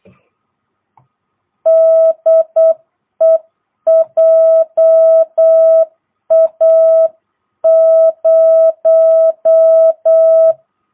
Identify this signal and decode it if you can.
Morse code, heard as a keyed tone, about 8 wpm, 660 Hz. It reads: DE JA0
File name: jh8cbh / 2022cw